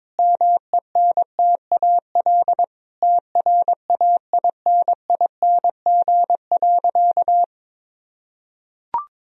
Morse code, heard as a keyed tone, about 22 wpm, 700 Hz. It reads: MENTAL TRAINING.
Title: Sentences from Top 1000 Words - Review Part 4 - 22wpm